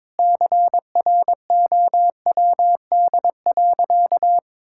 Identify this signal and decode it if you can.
Morse code, heard as a keyed tone, about 22 wpm, 700 Hz. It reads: CROWD.
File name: practice / 22wpm / Sentences with 4-Word Phrases - 22wpm